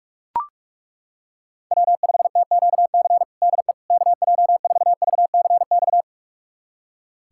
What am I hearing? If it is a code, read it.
Morse code, heard as a keyed tone, about 42 wpm, 700 Hz. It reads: W5TQC DE KJ4VCX
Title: USA Callsign DE Callsign 45wpm